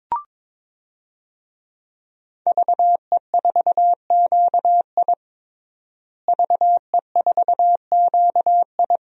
Morse code, heard as a keyed tone, about 22 wpm, 700 Hz. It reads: VE4QI VE4QI
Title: DXCC-100 Call Signs - 5 Characters - Repeated 3x - 22wpm